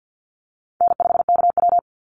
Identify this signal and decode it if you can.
Morse code, heard as a keyed tone, about 50 wpm, 700 Hz. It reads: N5KW
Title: Easy Call Signs Repeated 3x - 50wpm